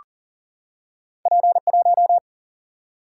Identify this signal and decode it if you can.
Morse code, heard as a keyed tone, about 40 wpm, 700 Hz. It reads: P1
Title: Single-Letter Single-Number 40wpm